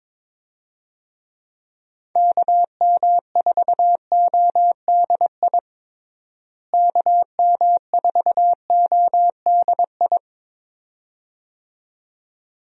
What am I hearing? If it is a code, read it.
Morse code, heard as a keyed tone, about 22 wpm, 700 Hz. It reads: KM4ODI KM4ODI
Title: USA Call Signs - 6 Characters - Repeated 2x - 22wpm